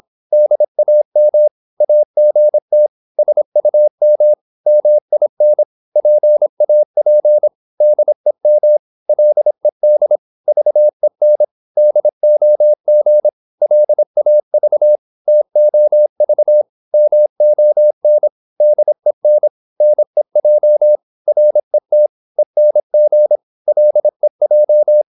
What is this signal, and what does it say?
Morse code, heard as a keyed tone, about 26 wpm, 600 Hz. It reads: DAM AGT SUM MIN PAP DEM LED VEN DOG LAV TOV MON DEN NEJ RET ENG LEJ
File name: Korte ord DK 26wpm